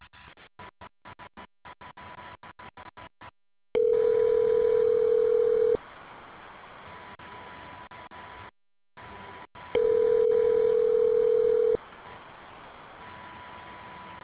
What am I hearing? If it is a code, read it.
Ringback tone (North America), 440 + 480 Hz.